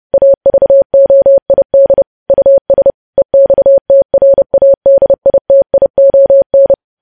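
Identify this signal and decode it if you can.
Morse code, heard as a keyed tone, about 30 wpm, 550 Hz. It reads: AVOID US EXTRADITION
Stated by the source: Morse practice of news headlines at 30wpm.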